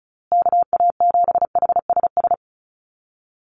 Morse code, heard as a keyed tone, about 34 wpm, 700 Hz. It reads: KA7HSS
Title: USA Call Signs Repeated 2x 35wpm